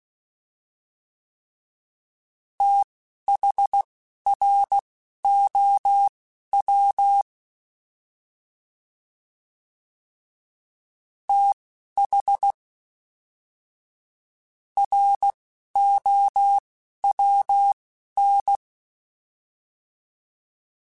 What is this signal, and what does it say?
Morse code, heard as a keyed tone, about 16 wpm, spaced as for 9 wpm, 800 Hz. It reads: THROW THROWN
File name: TH / TH mp3s / Fast TH @10